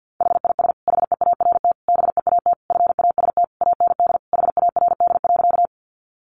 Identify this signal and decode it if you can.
Morse code, heard as a keyed tone, about 50 wpm, 700 Hz. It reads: HIS HEART BEAT FAST AND HARD.
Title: Sentences from Top 1300 Words 50wpm